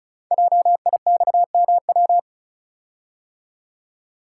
Morse code, heard as a keyed tone, about 36 wpm, 700 Hz. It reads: JIXMW
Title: Code Groups - 5 Letters - 35wpm